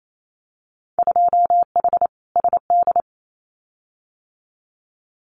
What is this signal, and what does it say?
Morse code, heard as a keyed tone, about 28 wpm, 700 Hz. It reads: 2H SD
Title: Winter Field Day Exchanges 28wpm